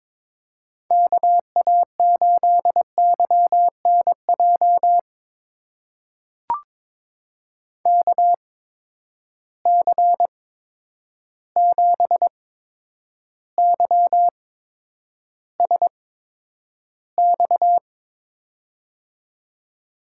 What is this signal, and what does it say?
Morse code, heard as a keyed tone, about 22 wpm, spaced as for 9 wpm, 700 Hz. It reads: KA8YNJ KC7YSX
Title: USA Call Signs - 6 Characters - 8x Character Spacing - 22wpm